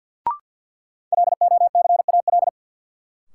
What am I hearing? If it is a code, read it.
Morse code, heard as a keyed tone, about 51 wpm, 700 Hz. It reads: LOCAL
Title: Sentences from Top 1500 Words - New Word Drill Alt - 50wpm